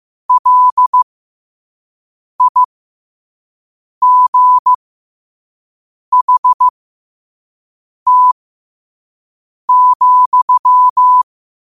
Morse code, heard as a keyed tone, about 15 wpm, spaced as for 5 wpm, 1000 Hz. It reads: LIGHT,
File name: qotdfiles / 5wpm